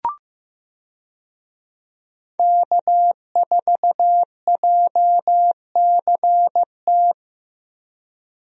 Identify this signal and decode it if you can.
Morse code, heard as a keyed tone, about 15 wpm, 700 Hz. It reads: K4JCT
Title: USA Call Signs - 5 Characters - Repeated 3x - 15wpm